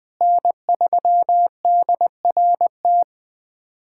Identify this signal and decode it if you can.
Morse code, heard as a keyed tone, about 19 wpm, 700 Hz. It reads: N3DRT
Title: USA Callsign DE Callsign - 5 Characters - 20wpm